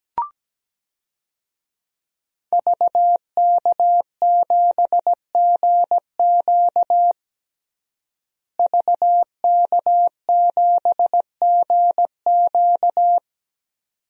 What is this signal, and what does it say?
Morse code, heard as a keyed tone, about 17 wpm, 700 Hz. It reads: VK7GQ VK7GQ
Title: DXCC-100 Call Signs - 5 Characters - Repeated 2x - 17wpm